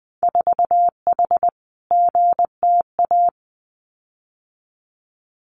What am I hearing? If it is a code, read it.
Morse code, heard as a keyed tone, about 19 wpm, 700 Hz. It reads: 4H GTA
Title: Winter Field Day Exchanges Encore 20wpm